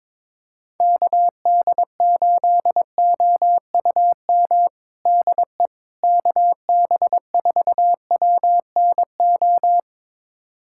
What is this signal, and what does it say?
Morse code, heard as a keyed tone, about 22 wpm, 700 Hz. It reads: KD8OUM DE KB4WNO
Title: USA Callsign DE Callsign 22wpm